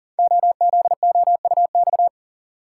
Morse code, heard as a keyed tone, about 42 wpm, 700 Hz. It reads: OZOUX
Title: Code Groups - 5 Letter:Numbers - 40wpm